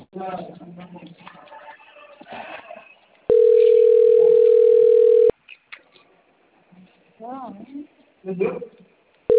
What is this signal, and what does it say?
Ringback tone (North America), 440 + 480 Hz, with speech.
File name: rg-1600-unknown- - - .35841